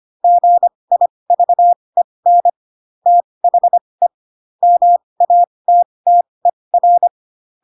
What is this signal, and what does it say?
Morse code, heard as a keyed tone, about 25 wpm, spaced as for 20 wpm, 700 Hz. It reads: GIVEN THE MATTER